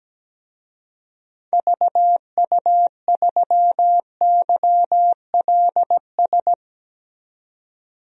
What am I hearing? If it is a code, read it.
Morse code, heard as a keyed tone, about 17 wpm, 700 Hz. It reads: VU3YLS
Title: DXCC-100 Call Signs - 6 Characters - 17wpm